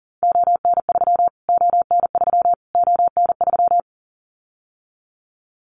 Morse code, heard as a keyed tone, about 40 wpm, 700 Hz. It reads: ON3 ON3 ON3
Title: Callsign Prefixes for Top 25 DXCC - Repeated 3x - 40wpm